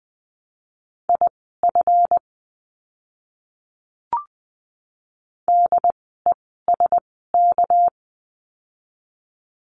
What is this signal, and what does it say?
Morse code, heard as a keyed tone, about 19 wpm, spaced as for 14 wpm, 700 Hz. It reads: IF DESK
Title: 199 Lesson 32f - Letter k - Words 20wpm